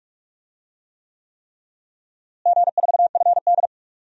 Morse code, heard as a keyed tone, about 48 wpm, 700 Hz. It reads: G4FD
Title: DXCC-100 Call Signs - 4 Characters - Repeated 3x - 45wpm